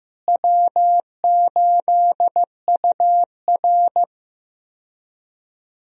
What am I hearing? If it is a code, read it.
Morse code, heard as a keyed tone, about 15 wpm, 700 Hz. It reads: W8UR
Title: USA Call Signs - 4 Characters - Repeated 2x - 15wpm